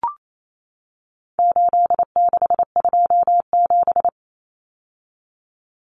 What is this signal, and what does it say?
Morse code, heard as a keyed tone, about 28 wpm, 700 Hz. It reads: 8627
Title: Numbers - 4-Digit 28wpm